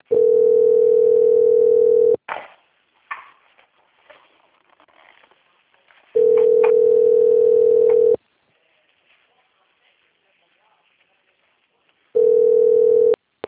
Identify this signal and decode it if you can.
Ringback tone (North America), 440 + 480 Hz.